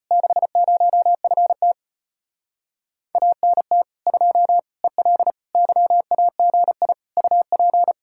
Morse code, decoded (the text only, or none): 60FT ANT 2 EL YAGI UP